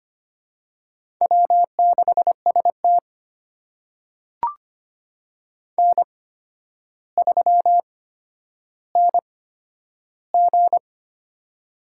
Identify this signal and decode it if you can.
Morse code, heard as a keyed tone, about 25 wpm, spaced as for 11 wpm, 700 Hz. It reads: W6ST N3NG
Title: USA Call Signs - 4 Characters - 8x Character Spacing - 25wpm